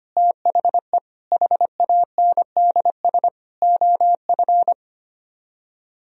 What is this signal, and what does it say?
Morse code, heard as a keyed tone, about 32 wpm, 700 Hz. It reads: THE HANDS OF